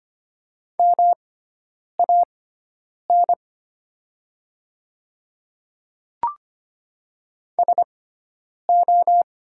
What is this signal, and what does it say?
Morse code, heard as a keyed tone, about 25 wpm, spaced as for 9 wpm, 690 Hz. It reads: MAN SO
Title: 6x Character Spacing - Top 200 Words - 25wpm